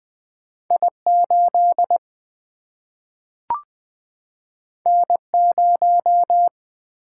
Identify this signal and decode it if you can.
Morse code, heard as a keyed tone, about 20 wpm, 700 Hz. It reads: I8 N0
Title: Single-Letter Single-Number 20wpm